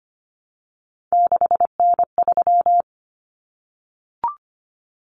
Morse code, heard as a keyed tone, about 25 wpm, 700 Hz. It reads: 6N3
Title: Common Cut Numbers - 3-Digit 25wpm